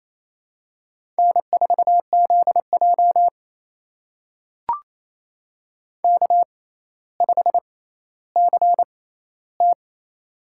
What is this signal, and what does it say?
Morse code, heard as a keyed tone, about 28 wpm, spaced as for 14 wpm, 700 Hz. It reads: N4ZJ K5CT
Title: USA Call Signs - 4 Characters - 6x Character Spacing - 28wpm